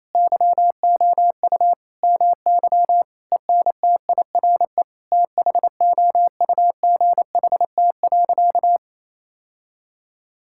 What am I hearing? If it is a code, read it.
Morse code, heard as a keyed tone, about 28 wpm, 700 Hz. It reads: YOU MY ENTIRE THOUGHT.
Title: Sentences from Top 1000 Words - Review Part 5 - 28wpm